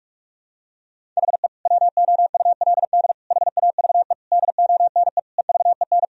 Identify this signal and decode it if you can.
Morse code, heard as a keyed tone, about 46 wpm, 700 Hz. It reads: HE WOULD HAVE DONE EVEN
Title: Sentences from Top 1000 Words - Review Part 1 - 45wpm